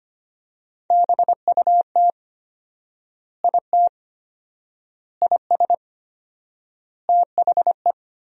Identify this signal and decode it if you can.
Morse code, heard as a keyed tone, about 25 wpm, 700 Hz. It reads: BUT IT IS THE
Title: 4x Word-Spacing - Sentences from Top 300 Words - 25wpm